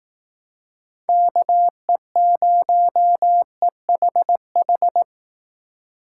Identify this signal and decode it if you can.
Morse code, heard as a keyed tone, about 18 wpm, 700 Hz. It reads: KE0EHH
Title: Speed Racing - USA 2x3 Call Signs 15wpm